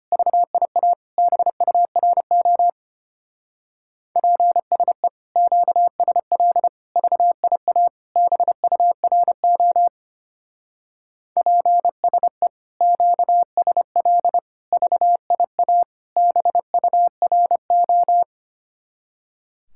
Morse code, heard as a keyed tone, about 29 wpm, 700 Hz. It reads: VIA BURO PSE QSL VIA BURO PSE QSL VIA BURO
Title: Speed-Racing Common QSO Phrases - 25wpm